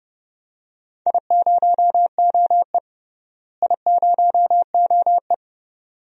Morse code, heard as a keyed tone, about 31 wpm, 700 Hz. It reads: I0OE I0OE
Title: DXCC-100 Call Signs - 4 Characters - Repeated 2x - 30wpm